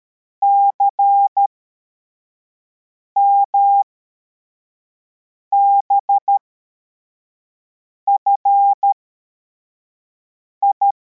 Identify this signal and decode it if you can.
Morse code, heard as a keyed tone, about 13 wpm, spaced as for 4 wpm, 800 Hz. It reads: CMBFI
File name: formation / cw / Text73